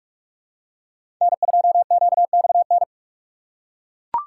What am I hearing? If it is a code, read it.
Morse code, heard as a keyed tone, about 42 wpm, 700 Hz. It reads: N2QXN
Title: USA Call Signs - 5 Characters - 4x Character Spacing - 45wpm